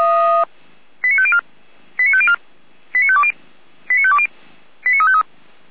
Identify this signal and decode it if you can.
Five-tone selective call (ZVEI), heard as quick runs of five tones, tones of about 70 ms. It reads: 89504 89504 89520 89520 89353